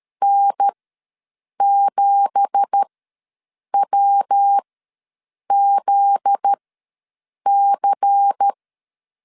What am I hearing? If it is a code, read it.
Morse code, heard as a keyed tone, about 13 wpm, spaced as for 7 wpm, 800 Hz. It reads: N7WZC
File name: LEZIONI / 10 SIGNS / 10A SLASH ws 12wpm 3443